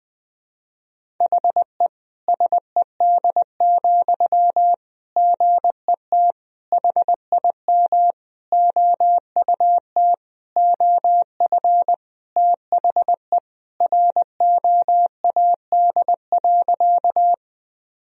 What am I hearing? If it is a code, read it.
Morse code, heard as a keyed tone, about 20 wpm, 700 Hz. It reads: HE SED, GET HIM OUT OF THE ROAD.
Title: Sentences from Top 400 Words and Abbreviations 20wpm